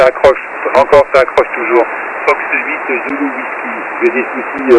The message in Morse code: none